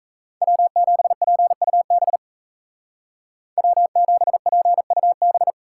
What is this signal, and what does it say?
Morse code, heard as a keyed tone, about 37 wpm, 700 Hz. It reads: W7PUB W7PUB